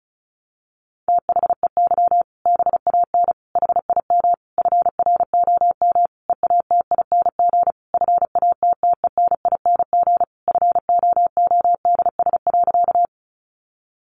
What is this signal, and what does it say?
Morse code, heard as a keyed tone, about 34 wpm, 700 Hz. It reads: THEY BAN HIM FROM EATING FATTENING FOODS.
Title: Easy Phrases 35wpm